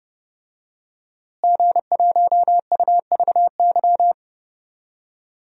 Morse code, heard as a keyed tone, about 30 wpm, 700 Hz. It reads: G1UVY
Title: DXCC-100 Call Signs - 5 Characters - Repeated 2x - 30wpm